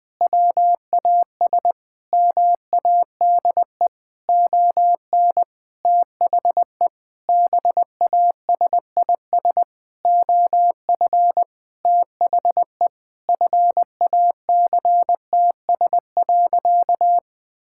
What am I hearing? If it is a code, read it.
Morse code, heard as a keyed tone, about 20 wpm, 700 Hz. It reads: WAS MADE ON THE BASIS OF THE FACTS.
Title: Sentences with 5-Word Phrases - 20wpm